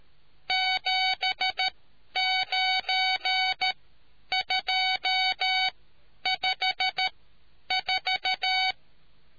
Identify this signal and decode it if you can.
Morse code, heard as a keyed tone, about 13 wpm, spaced as for 9 wpm, 750 Hz. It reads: 79254